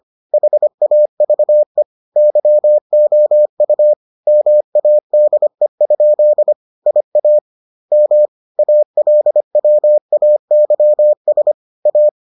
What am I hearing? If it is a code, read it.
Morse code, heard as a keyed tone, about 25 wpm, 600 Hz. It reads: HAVE YOU MADE? IA M ALWAYS A